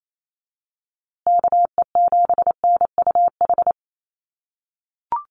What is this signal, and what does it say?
Morse code, heard as a keyed tone, about 28 wpm, 700 Hz. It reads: KE7NUH
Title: USA Call Signs Repeated 3x 28wpm